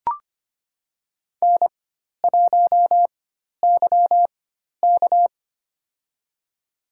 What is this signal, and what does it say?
Morse code, heard as a keyed tone, about 25 wpm, spaced as for 12 wpm, 700 Hz. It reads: N1YK
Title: USA Call Signs - 4x Character Spacing - 25wpm